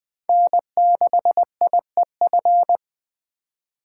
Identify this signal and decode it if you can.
Morse code, heard as a keyed tone, about 20 wpm, 700 Hz. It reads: N6IEF
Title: Speed Racing - USA Call Signs 17wpm